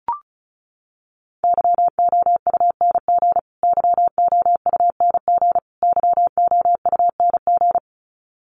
Morse code, heard as a keyed tone, about 34 wpm, 700 Hz. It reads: YOUNG YOUNG YOUNG
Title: Top 500 Words Repeated 35wpm